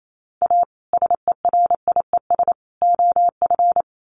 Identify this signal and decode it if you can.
Morse code, heard as a keyed tone, about 28 wpm, 700 Hz. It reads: A SERIES OF
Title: Sets of 3 Words 28wpm